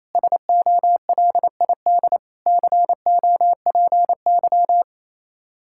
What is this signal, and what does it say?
Morse code, heard as a keyed tone, about 27 wpm, 700 Hz. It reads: SOLID COPY
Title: Practice QSO - Exchange 3 - More Info - 28wpm